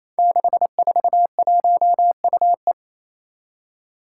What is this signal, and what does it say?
Morse code, heard as a keyed tone, about 27 wpm, 700 Hz. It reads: 641UE
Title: Five Challenging Letter-Number Combinations 28wpm